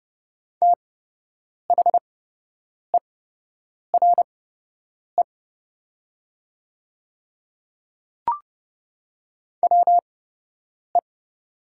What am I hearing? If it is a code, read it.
Morse code, heard as a keyed tone, about 30 wpm, spaced as for 8 wpm, 700 Hz. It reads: THERE WE
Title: 8x Character Spacing - Top 100 Words - 30wpm